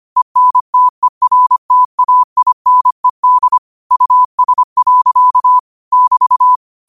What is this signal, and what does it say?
Morse code, heard as a keyed tone, about 25 wpm, 1000 Hz. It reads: ENTERTAINED US. <BT>
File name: qotdfiles / 25wpm